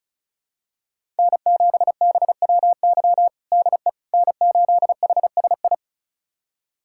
Morse code, heard as a keyed tone, about 36 wpm, 700 Hz. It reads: N7BWY DE N8HSI
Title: USA Callsign DE Callsign - 5 Characters - 35wpm